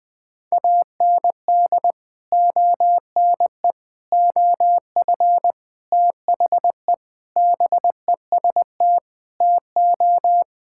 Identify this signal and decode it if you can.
Morse code, heard as a keyed tone, about 21 wpm, 700 Hz. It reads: AND ONE OF THE BEST TO
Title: Sentences with 4-Word Phrases - 20wpm